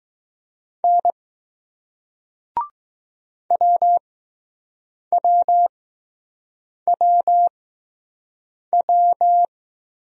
Morse code, heard as a keyed tone, about 20 wpm, spaced as for 6 wpm, 700 Hz. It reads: N WWWW